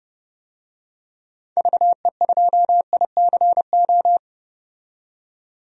Morse code, heard as a keyed tone, about 30 wpm, 700 Hz. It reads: VE2ICO
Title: DXCC-100 Call Signs - 6 Characters - 30wpm